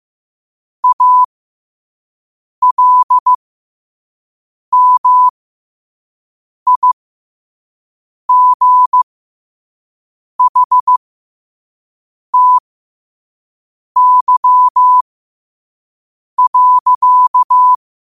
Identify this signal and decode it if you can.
Morse code, heard as a keyed tone, about 15 wpm, spaced as for 5 wpm, 1000 Hz. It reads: ALMIGHTY.
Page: New quotes every day in morse code at 5 Words per minute.